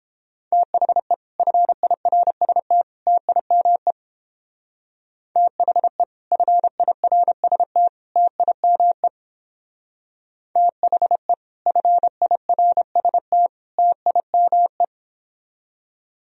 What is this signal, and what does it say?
Morse code, heard as a keyed tone, about 29 wpm, 700 Hz. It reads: THE FIRST TIME THE FIRST TIME THE FIRST TIME